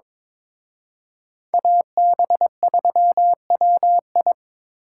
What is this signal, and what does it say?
Morse code, heard as a keyed tone, about 26 wpm, 700 Hz. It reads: AB3WI
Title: Speed Racing - USA 2x2 Call Signs 22wpm